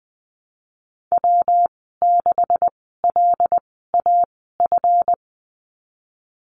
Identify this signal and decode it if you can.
Morse code, heard as a keyed tone, about 20 wpm, spaced as for 15 wpm, 700 Hz. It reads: W6LAF
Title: 147 Lesson 24g - Letter y - Call Signs 20wpm